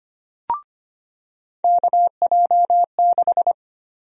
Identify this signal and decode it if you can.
Morse code, heard as a keyed tone, about 25 wpm, 700 Hz. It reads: KJ6
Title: USA Call Sign Prefixes - 25wpm